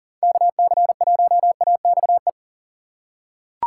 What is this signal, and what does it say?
Morse code, heard as a keyed tone, about 39 wpm, 700 Hz. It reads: KC1AXE
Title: USA Call Signs - 6 Characters - 2x Character Spacing - 40wpm